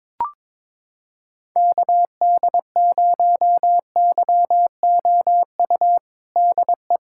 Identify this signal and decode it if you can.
Morse code, heard as a keyed tone, about 21 wpm, 700 Hz. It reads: KD0YOU DE
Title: USA Callsign DE Callsign - 6 Characters - 22wpm